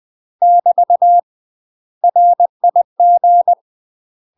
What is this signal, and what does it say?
Morse code, heard as a keyed tone, about 20 wpm, 700 Hz. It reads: <BT> RIG
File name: UK MASTER QSOs TRAINING / 20WPM / UK MASTER QSOs 0046